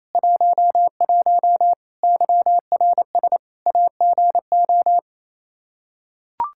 Morse code, heard as a keyed tone, about 28 wpm, 700 Hz. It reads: 11 YRS AGO
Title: Practice QSO - Exchange 3 - More Info - 28wpm